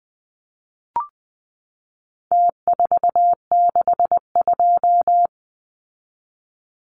Morse code, heard as a keyed tone, about 20 wpm, 700 Hz. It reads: T462
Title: Numbers - Contest Style - 4-Digit 20wpm